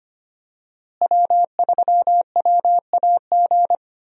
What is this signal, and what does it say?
Morse code, heard as a keyed tone, about 25 wpm, 700 Hz. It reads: W3WAG
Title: USA Call Signs - 5 Characters - Repeated 2x - 25wpm